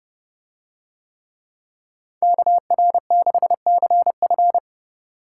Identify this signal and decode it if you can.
Morse code, heard as a keyed tone, about 29 wpm, 700 Hz. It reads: KR6CF